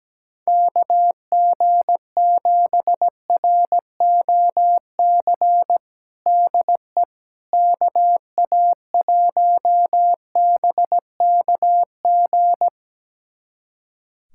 Morse code, heard as a keyed tone, about 17 wpm, 700 Hz. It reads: KG7ROC DE KA1BKG
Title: USA Callsign DE Callsign - 6 Characters - 17wpm